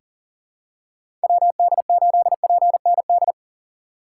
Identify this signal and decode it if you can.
Morse code, heard as a keyed tone, about 40 wpm, 700 Hz. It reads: WD8PND
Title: USA Call Signs - 6 Characters - 6x Character Spacing - 40wpm